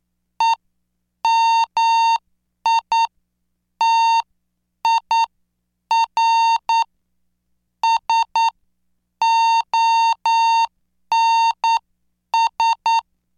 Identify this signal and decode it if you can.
Morse code, heard as a keyed tone, about 9 wpm, spaced as for 7 wpm, 910 Hz. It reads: EMITIR SONS